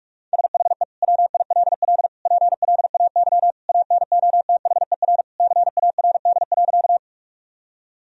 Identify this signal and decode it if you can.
Morse code, heard as a keyed tone, about 46 wpm, 700 Hz. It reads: SHE WILL PLAY ANOTHER CARD.
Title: Sentences from Top 1000 Words - Review Part 4 - 45wpm